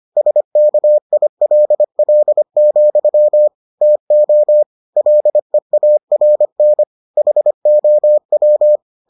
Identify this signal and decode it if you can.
Morse code, heard as a keyed tone, about 25 wpm, 600 Hz. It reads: SKILL, TO LEARN HOW